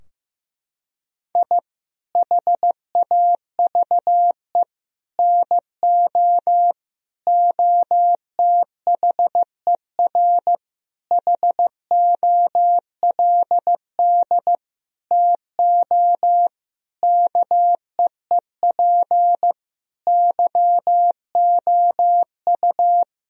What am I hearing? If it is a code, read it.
Morse code, heard as a keyed tone, about 16 wpm, 700 Hz. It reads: I HAVE NO OTHER HOLD TO KEEP YOU
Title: Sentences from Top 1000 Words - Review Part 2 - 15wpm